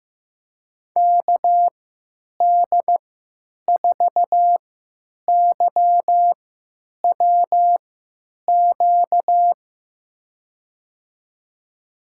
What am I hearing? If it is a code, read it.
Morse code, heard as a keyed tone, about 15 wpm, spaced as for 9 wpm, 700 Hz. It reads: KD4YWQ
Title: USA Call Signs - 6 Characters - 3x Character Spacing - 15wpm